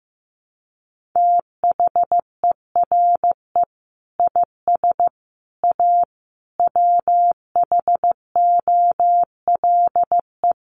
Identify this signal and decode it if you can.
Morse code, heard as a keyed tone, about 15 wpm, 700 Hz. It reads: THERE IS A WHOLE
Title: Sentences from Top 1400 Words 15wpm